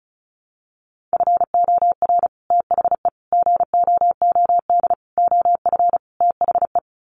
Morse code, heard as a keyed tone, about 34 wpm, 700 Hz. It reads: FOR THE GOOD OF THE
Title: Sets of 5 Words Encore2 35wpm